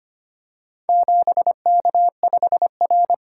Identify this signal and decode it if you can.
Morse code, heard as a keyed tone, about 25 wpm, 700 Hz. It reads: 7K5R
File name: practice / 25wpm / Four Letter-Number Combinations - 25wpm